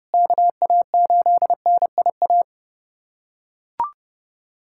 Morse code, heard as a keyed tone, about 32 wpm, 690 Hz. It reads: KA8NIA
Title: USA Callsign DE Callsign - 6 Characters - 30wpm